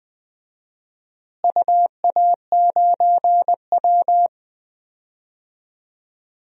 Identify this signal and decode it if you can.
Morse code, heard as a keyed tone, about 20 wpm, 700 Hz. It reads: UA9W
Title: DXCC-100 Call Signs - 4 Characters - Repeated 2x - 20wpm